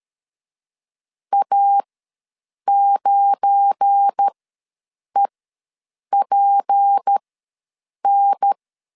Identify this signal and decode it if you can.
Morse code, heard as a keyed tone, about 13 wpm, spaced as for 7 wpm, 800 Hz. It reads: A9EPN
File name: LEZIONI / 8 YJ45 / 8 YJ4 ws 12wpm 3249